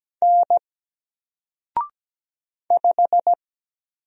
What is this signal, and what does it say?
Morse code, heard as a keyed tone, about 17 wpm, 700 Hz. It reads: N 5
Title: Single Letter-Number 17wpm